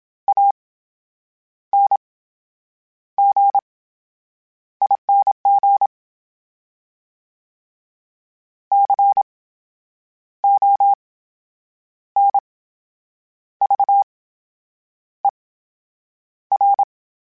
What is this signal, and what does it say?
Morse code, heard as a keyed tone, about 27 wpm, spaced as for 8 wpm, 800 Hz. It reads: ANGING CONVER
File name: Special INGs / Special INGs-mp3s / VerySlowToFaster Ing @25